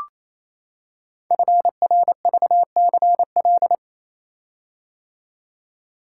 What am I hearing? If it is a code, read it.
Morse code, heard as a keyed tone, about 27 wpm, 700 Hz. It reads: FRVCL
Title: Code Groups - 5 Letters - 28wpm